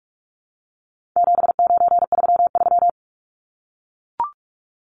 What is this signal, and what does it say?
Morse code, heard as a keyed tone, about 44 wpm, 700 Hz. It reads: 7933
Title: Numbers - 4-Digit 45wpm